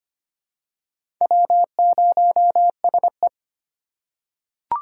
Morse code, heard as a keyed tone, about 25 wpm, 700 Hz. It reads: W0SE
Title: USA Call Signs - 4 Characters - 3x Character Spacing - 25wpm